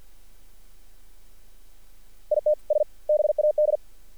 Morse code, heard as a keyed tone, about 48 wpm, 600 Hz. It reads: ITS BAD
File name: archived / CRSnet